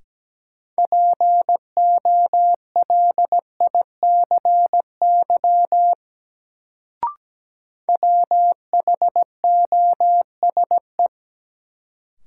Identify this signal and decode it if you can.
Morse code, heard as a keyed tone, about 17 wpm, 700 Hz. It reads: POLICY WHOSE
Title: Sentences from Top 1200 Words - New Word Drill Alt - 17wpm